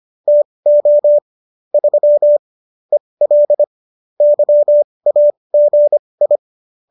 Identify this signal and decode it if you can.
Morse code, heard as a keyed tone, about 25 wpm, spaced as for 20 wpm, 590 Hz. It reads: TO 3 EL YAGI